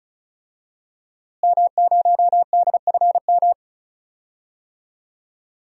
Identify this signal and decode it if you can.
Morse code, heard as a keyed tone, about 36 wpm, 700 Hz. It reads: M0DFM
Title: DXCC-100 Call Signs - 5 Characters - Repeated 3x - 35wpm